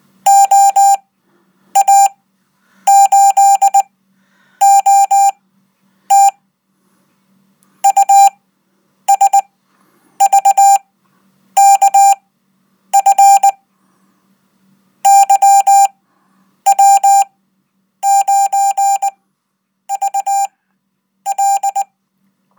Morse code, heard as a keyed tone, about 20 wpm, spaced as for 9 wpm, 790 Hz. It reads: OA8OT USVKF YW9VL
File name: Code